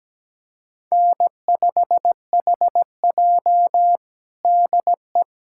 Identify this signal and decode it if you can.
Morse code, heard as a keyed tone, about 17 wpm, 700 Hz. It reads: N5HJ DE
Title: USA Callsign DE Callsign - 4 Characters - 17wpm